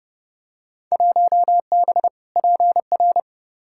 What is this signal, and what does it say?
Morse code, heard as a keyed tone, about 31 wpm, 700 Hz. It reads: 1B PR
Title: Field Day Exchanges Encore - 30wpm